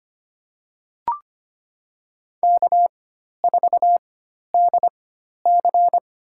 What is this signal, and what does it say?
Morse code, heard as a keyed tone, about 25 wpm, spaced as for 12 wpm, 700 Hz. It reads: K4DC
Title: USA Call Signs - 4 Characters - 4x Character Spacing - 25wpm